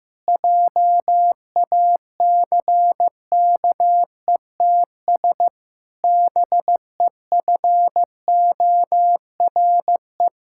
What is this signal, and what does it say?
Morse code, heard as a keyed tone, about 15 wpm, 700 Hz. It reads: JACKETS BEFORE
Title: Pangram Challenge v2 - 15wpm